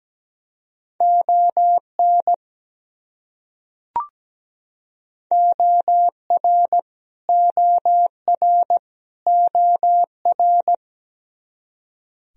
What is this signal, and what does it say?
Morse code, heard as a keyed tone, about 17 wpm, 700 Hz. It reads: ON OR OR OR
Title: Top 100 Words Repeated 17wpm